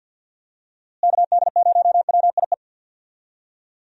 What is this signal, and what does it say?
Morse code, heard as a keyed tone, about 47 wpm, 700 Hz. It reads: KD0WIE
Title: USA Call Signs - 6 Characters - Repeated 2x - 50wpm